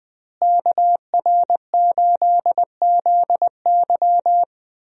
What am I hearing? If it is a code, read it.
Morse code, heard as a keyed tone, about 21 wpm, 700 Hz. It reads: KR8ZY
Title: USA Call Signs - 5 Characters - Repeated 3x - 20wpm